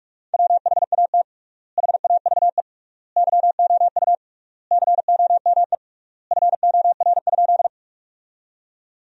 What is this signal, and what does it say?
Morse code, heard as a keyed tone, about 46 wpm, 700 Hz. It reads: WHAT HAVE YOU COME FOR?
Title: 3x Word-Spacing - Sentences from Top 100 Words - 45wpm